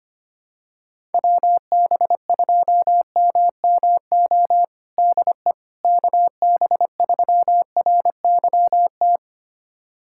Morse code, heard as a keyed tone, about 25 wpm, 700 Hz. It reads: WB2MMO DE KB3RYT
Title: USA Callsign DE Callsign - 6 Characters - 25wpm